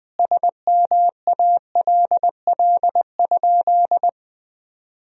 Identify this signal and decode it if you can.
Morse code, heard as a keyed tone, about 20 wpm, 700 Hz. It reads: SMALL?
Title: 2x Word-Spacing - Sentences from Top 200 Words - 20wpm